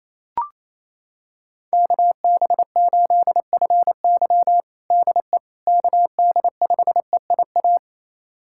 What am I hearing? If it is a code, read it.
Morse code, heard as a keyed tone, about 28 wpm, 700 Hz. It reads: KB8FY DE KD5EIA
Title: USA Callsign DE Callsign 28wpm